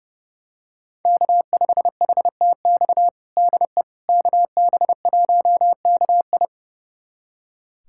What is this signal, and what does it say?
Morse code, heard as a keyed tone, about 32 wpm, 700 Hz. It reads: K5HTX DE KB1KI
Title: USA Callsign DE Callsign - 5 Characters - 30wpm